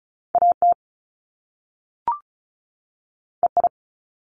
Morse code, heard as a keyed tone, about 36 wpm, 700 Hz. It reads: AT EI
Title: Top Two Letter Combinations in Words 35wpm